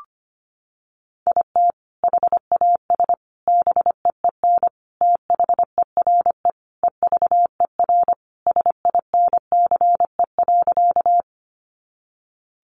Morse code, heard as a keyed tone, about 25 wpm, 700 Hz. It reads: IT HAS BEEN THERE EVER SINCE.
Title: Sentences from Top 400 Words 25wpm